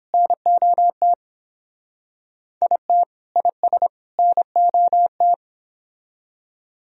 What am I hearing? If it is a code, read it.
Morse code, heard as a keyed tone, about 28 wpm, 700 Hz. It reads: NOT IT IS NOT